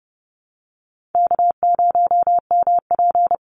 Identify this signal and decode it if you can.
Morse code, heard as a keyed tone, about 30 wpm, 700 Hz. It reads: K0MP
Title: NA SOTA Chaser Callsigns 30wpm